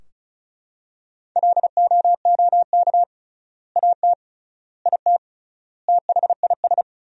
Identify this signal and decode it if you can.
Morse code, heard as a keyed tone, about 34 wpm, 700 Hz. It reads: LOOK AT IT THIS
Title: 3x Word-Spacing - Sentences from Top 100 Words - 35wpm